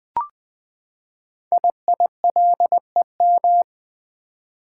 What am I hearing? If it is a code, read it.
Morse code, heard as a keyed tone, about 20 wpm, 700 Hz. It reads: IILEM
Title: Code Groups - 5 Letter:Numbers - 20wpm